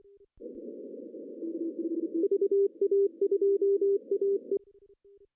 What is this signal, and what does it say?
Morse code, heard as a keyed tone, about 25 wpm, 400 Hz. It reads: VA2AE
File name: CW / cqww-cw-k5tr